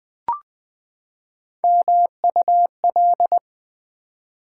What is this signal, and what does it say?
Morse code, heard as a keyed tone, about 20 wpm, 700 Hz. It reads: MUL
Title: Build A Word - 3 Letters At A Time - 8-Letter Words - 20wpm